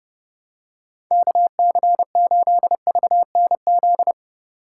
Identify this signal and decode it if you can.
Morse code, heard as a keyed tone, about 30 wpm, 700 Hz. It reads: KC8VNZ
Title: USA Call Signs - 3x Character Spacing - 30wpm